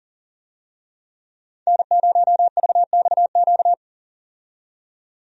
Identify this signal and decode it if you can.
Morse code, heard as a keyed tone, about 40 wpm, 700 Hz. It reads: N0VXQ
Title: USA Call Signs - 5 Characters - 40wpm